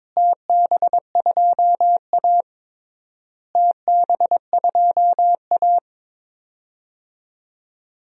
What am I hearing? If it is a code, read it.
Morse code, heard as a keyed tone, about 22 wpm, 700 Hz. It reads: TB2A TB2A
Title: DXCC-100 Call Signs - 4 Characters - Repeated 3x - 22wpm